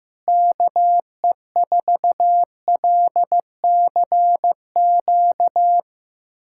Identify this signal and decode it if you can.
Morse code, heard as a keyed tone, about 16 wpm, 700 Hz. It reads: KE4LCQ
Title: USA Callsign DE Callsign - 6 Characters - 15wpm